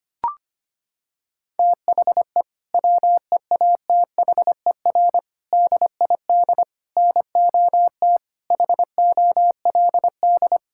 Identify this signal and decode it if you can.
Morse code, heard as a keyed tone, about 25 wpm, 700 Hz. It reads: THE WEATHER DID NOT HOLD
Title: Sentences from Top 1000 Words - Review Part 5 - 25wpm